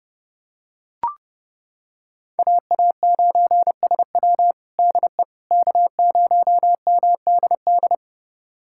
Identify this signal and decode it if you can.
Morse code, heard as a keyed tone, about 30 wpm, 700 Hz. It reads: AA9SW DE K0MDD
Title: USA Callsign DE Callsign 30wpm